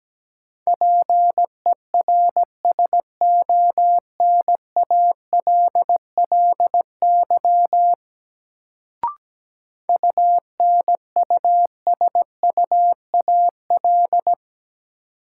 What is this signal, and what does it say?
Morse code, heard as a keyed tone, about 17 wpm, 700 Hz. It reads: PERSONALLY UNUSUAL
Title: Sentences from Top 1400 Words - New Word Drill - 17wpm